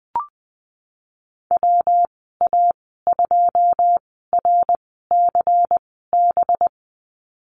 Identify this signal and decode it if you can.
Morse code, heard as a keyed tone, about 20 wpm, spaced as for 15 wpm, 700 Hz. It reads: WA2RCB
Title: 193 Lesson 31g - Letter v - Call Signs 20wpm